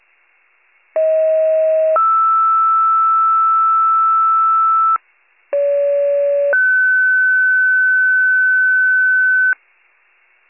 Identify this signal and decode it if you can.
Two-tone page, heard as a one-second tone, then a three-second tone, about 640 then 1320 Hz, about 570 then 1500 Hz. It is heard from 1.0 to 9.5 s.